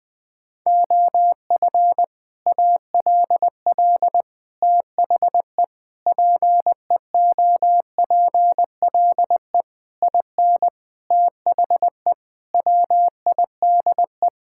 Morse code, heard as a keyed tone, about 20 wpm, 700 Hz. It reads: OF ALL THE PEOPLE IN THE WIDE
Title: Sentences from Top 1000 Words - Review Part 6 - 20wpm